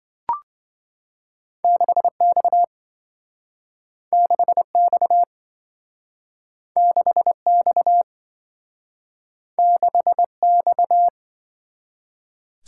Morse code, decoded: 6X 6X 6X 6X

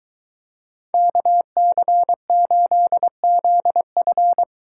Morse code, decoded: KC8ZF

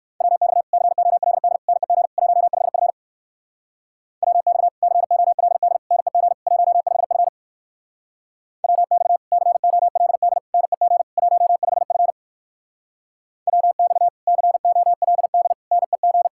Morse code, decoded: WX COLD NEG 15F WX COLD NEG 15F WX COLD NEG 15F WX COLD NEG

